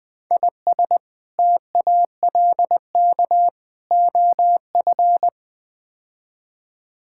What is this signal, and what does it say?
Morse code, heard as a keyed tone, about 20 wpm, 700 Hz. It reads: IS TALK OF
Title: Build A Phrase - 5-Word Phrases - 20wpm